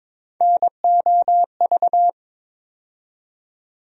Morse code, heard as a keyed tone, about 22 wpm, 700 Hz. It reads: NOV
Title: USA Call Sign Postfixes - 3 Letters 22wpm